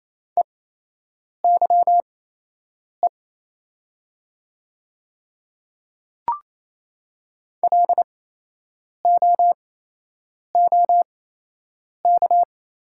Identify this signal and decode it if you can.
Morse code, heard as a keyed tone, about 28 wpm, spaced as for 8 wpm, 690 Hz. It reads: EYE LOOK
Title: 8x Character Spacing - Top 300 Words - 28wpm